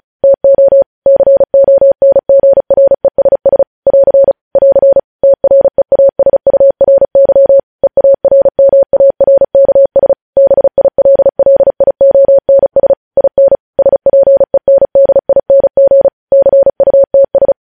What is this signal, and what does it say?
Morse code, heard as a keyed tone, about 36 wpm, 550 Hz. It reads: TO CONGRESS <AR> <AR> TREASURY EARMARKS BILLIONS IN SPENDING CUTS
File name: podcast / News headlines-1x-35x35